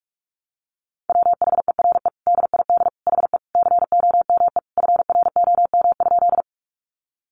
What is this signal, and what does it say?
Morse code, heard as a keyed tone, about 44 wpm, 700 Hz. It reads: WHERE DID HE COME FROM?
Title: Sentences from Top 200 Words 45wpm